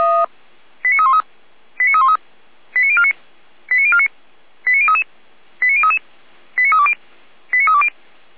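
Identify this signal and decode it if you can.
Five-tone selective call (ZVEI), heard as quick runs of five tones, tones of about 70 ms. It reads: 89313 89313 89049 89049 89033 89033 89320 89320
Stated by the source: ZVEI five-tone call received by SDR, from the audio server of Feuerwehr Orlamünde (f85360).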